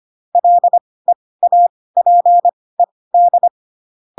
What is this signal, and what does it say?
Morse code, heard as a keyed tone, about 25 wpm, spaced as for 18 wpm, 700 Hz. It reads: LEAPED
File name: Pinocchio / Pinocchio ch4 12 25 ik1hge